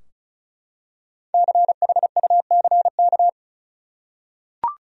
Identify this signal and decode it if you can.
Morse code, heard as a keyed tone, about 34 wpm, 700 Hz. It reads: CHUCK
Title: USA Common Names - Max 5 Letters - 35wpm